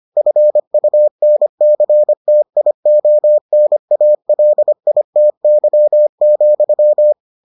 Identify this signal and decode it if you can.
Morse code, heard as a keyed tone, about 25 wpm, 600 Hz. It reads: FUNCTIONALITY,